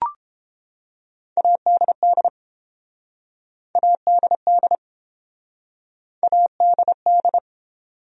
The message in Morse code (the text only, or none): ADD ADD ADD